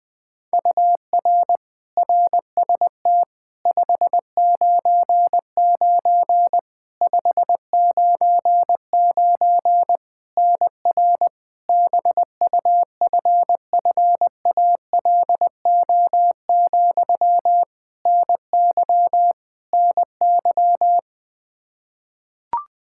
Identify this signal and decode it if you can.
Morse code, heard as a keyed tone, about 21 wpm, 700 Hz. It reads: UR RST 599 599 NR BUFFALO, NY NY
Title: K1USN SST QSO Practice 20wpm